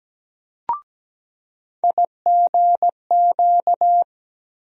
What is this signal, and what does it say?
Morse code, heard as a keyed tone, about 17 wpm, 700 Hz. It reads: IGQ
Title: USA Call Sign Postfixes - 3 Letters 17wpm